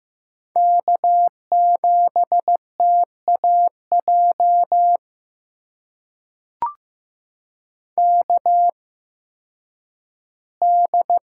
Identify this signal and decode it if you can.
Morse code, heard as a keyed tone, about 15 wpm, spaced as for 10 wpm, 700 Hz. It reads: K7TAJ KD
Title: USA Call Signs - 5 Characters - 8x Character Spacing - 15wpm